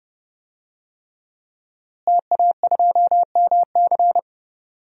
Morse code, heard as a keyed tone, about 30 wpm, 700 Hz. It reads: TA2MC
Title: DXCC-100 Call Signs - 5 Characters - Repeated 3x - 30wpm